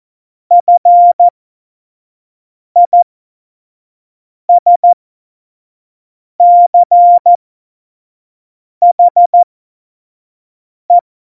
Morse code, heard as a keyed tone, about 14 wpm, spaced as for 5 wpm, 700 Hz. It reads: FISCHE
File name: Tempo 25 F70